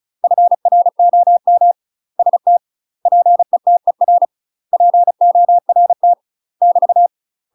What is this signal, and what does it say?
Morse code, heard as a keyed tone, about 34 wpm, 700 Hz. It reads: FROM ST PETER PORT <BT>
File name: UK MASTER QSOs TRAINING / 35WPM / UK MASTER QSOs 0026